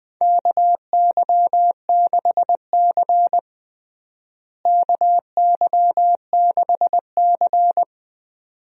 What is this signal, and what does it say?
Morse code, heard as a keyed tone, about 21 wpm, 700 Hz. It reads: KY6C KY6C
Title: USA Call Signs - 4 Characters - Repeated 2x - 20wpm